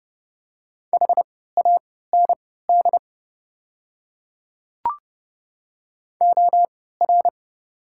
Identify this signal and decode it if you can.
Morse code, heard as a keyed tone, about 30 wpm, spaced as for 17 wpm, 700 Hz. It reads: HAND OR
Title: 3x Character Spacing - Top 200 Words - 30wpm